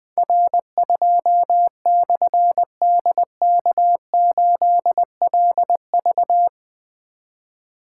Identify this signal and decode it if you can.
Morse code, heard as a keyed tone, about 20 wpm, 700 Hz. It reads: R2/DK8LV
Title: Difficult Call Signs 20wpm